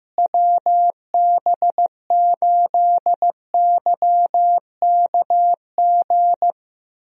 Morse code, heard as a keyed tone, about 16 wpm, 700 Hz. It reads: WB8YKG
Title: USA Call Signs - 6 Characters - Repeated 2x - 15wpm